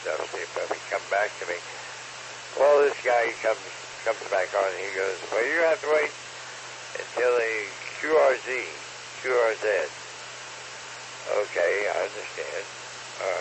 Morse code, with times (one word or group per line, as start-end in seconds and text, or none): none